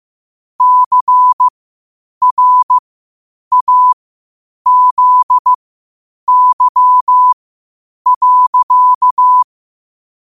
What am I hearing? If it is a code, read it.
Morse code, heard as a keyed tone, about 15 wpm, spaced as for 8 wpm, 1000 Hz. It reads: CRAZY.
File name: qotdfiles / 8wpm